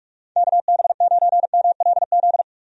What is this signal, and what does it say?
Morse code, heard as a keyed tone, about 43 wpm, 700 Hz. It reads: KB9MLZ
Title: USA Call Signs - 6 Characters - 45wpm